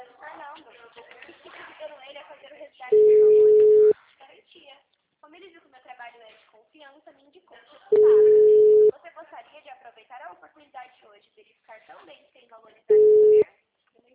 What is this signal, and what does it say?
Ringback tone (Europe, 425 Hz), with speech.